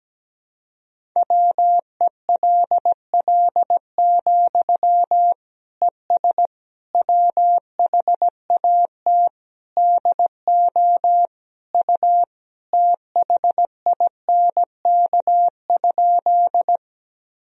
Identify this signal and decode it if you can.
Morse code, heard as a keyed tone, about 17 wpm, 700 Hz. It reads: WELL, ES WHAT DO U THINK?
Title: Sentences from Top 200 Words and Abbreviations 17wpm